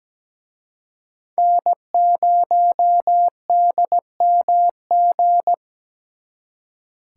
Morse code, heard as a keyed tone, about 20 wpm, 700 Hz. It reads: N0DMG
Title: Speed Racing - USA Call Signs 17wpm